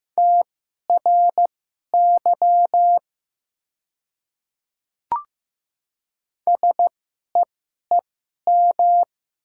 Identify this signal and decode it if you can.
Morse code, heard as a keyed tone, about 15 wpm, spaced as for 11 wpm, 700 Hz. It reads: TRY SEEM
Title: 2x Character Spacing - Top 300 Words - 15wpm